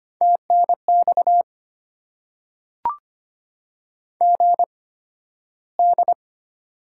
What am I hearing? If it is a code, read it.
Morse code, heard as a keyed tone, about 25 wpm, spaced as for 13 wpm, 700 Hz. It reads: TNX GD
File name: Top 90 QSO Elements - 8x Character Spacing - 25wpm